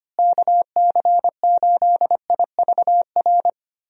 Morse code, heard as a keyed tone, about 25 wpm, 700 Hz. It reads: KC8IVR
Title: USA Call Signs - 6 Characters - Repeated 2x - 25wpm